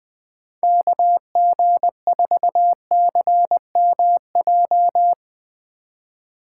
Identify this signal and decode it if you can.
Morse code, heard as a keyed tone, about 20 wpm, 700 Hz. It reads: KG4CMJ
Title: USA Call Signs - 6 Characters - 2x Character Spacing - 20wpm